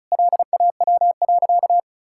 Morse code, heard as a keyed tone, about 34 wpm, 700 Hz. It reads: LAW.
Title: Sentences from Top 1000 Words - Review Part 6 - 35wpm